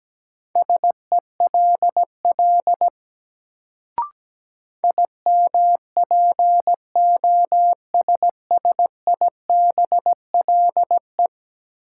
Morse code, heard as a keyed tone, about 17 wpm, 700 Hz. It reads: SELL IMPOSSIBLE
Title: Sentences from Top 1100 Words - New Word Drill Alt - 17wpm